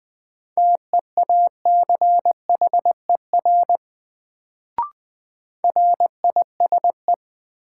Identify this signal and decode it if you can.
Morse code, heard as a keyed tone, about 20 wpm, 700 Hz. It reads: TEACHER RISE
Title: Sentences from Top 900 Words - New Word Drill - 20wpm